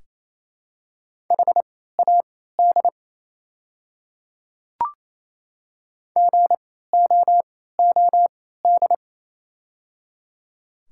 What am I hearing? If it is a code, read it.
Morse code, heard as a keyed tone, about 27 wpm, spaced as for 16 wpm, 700 Hz. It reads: HAD GOOD
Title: 3x Character Spacing - Top 200 Words - 28wpm